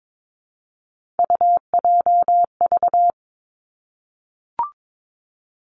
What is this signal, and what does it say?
Morse code, heard as a keyed tone, about 22 wpm, 700 Hz. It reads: UJV
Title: Three Challenging Letter-Number Combinations 22wpm